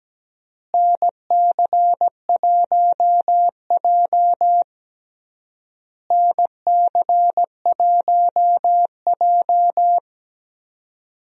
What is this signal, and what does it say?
Morse code, heard as a keyed tone, about 17 wpm, 700 Hz. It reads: NC1J NC1J
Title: USA Call Signs - 4 Characters - Repeated 3x - 17wpm